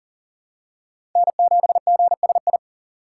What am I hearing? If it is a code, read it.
Morse code, heard as a keyed tone, about 40 wpm, 700 Hz. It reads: N7GSI
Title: USA Call Signs - 5 Characters - 2x Character Spacing - 40wpm